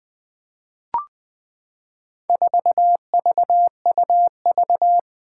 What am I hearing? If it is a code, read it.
Morse code, heard as a keyed tone, about 21 wpm, 700 Hz. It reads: 4VUV
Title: Four Challenging Letter-Number Combinations 20wpm